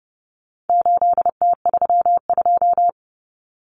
Morse code, heard as a keyed tone, about 30 wpm, 700 Hz. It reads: 8T32
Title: Common Cut Numbers - 4-Digit 30wpm